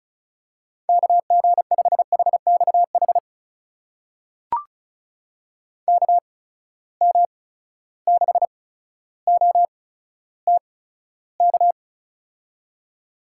Morse code, heard as a keyed tone, about 36 wpm, spaced as for 15 wpm, 700 Hz. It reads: KG5HXH KM6OTK
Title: USA Call Signs - 6 Characters - 8x Character Spacing - 35wpm